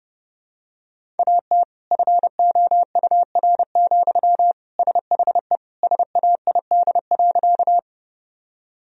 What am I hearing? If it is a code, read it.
Morse code, heard as a keyed tone, about 32 wpm, 700 Hz. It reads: AT FOUR, SHE SAID.
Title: Sentences from Top 1000 Words - Review Part 1 - 30wpm